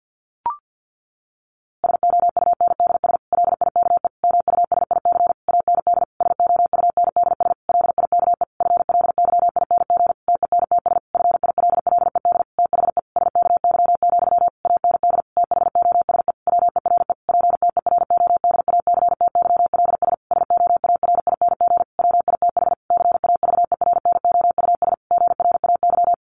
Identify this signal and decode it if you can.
Morse code, heard as a keyed tone, about 48 wpm, 700 Hz. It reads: SOUNDS LIKE MUSIC AND SOUNDS LIKE FLYING TENTS FILLED THE SKY, AND THOSE WERE PTERODACTYLS SOARING WITH CAVERNOUS GRAY
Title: A Sound Of Thunder 50wpm